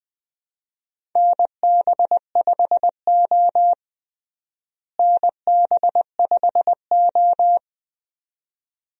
Morse code, heard as a keyed tone, about 21 wpm, 700 Hz. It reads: NB5O NB5O
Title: USA Call Signs - 4 Characters - Repeated 2x - 20wpm